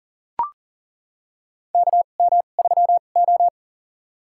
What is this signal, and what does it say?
Morse code, heard as a keyed tone, about 38 wpm, spaced as for 28 wpm, 700 Hz. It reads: KM3O
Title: USA Call Signs - 4 Characters - 2x Character Spacing - 40wpm